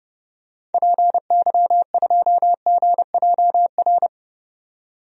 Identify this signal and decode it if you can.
Morse code, heard as a keyed tone, about 31 wpm, 700 Hz. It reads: PY2GJR
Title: DXCC-100 Call Signs - 6 Characters - 30wpm